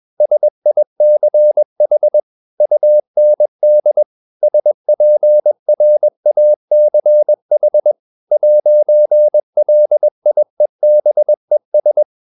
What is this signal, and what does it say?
Morse code, heard as a keyed tone, about 21 wpm, 600 Hz. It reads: SICH UND SPRACH 'LIEBES
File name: marienkind2 21wpm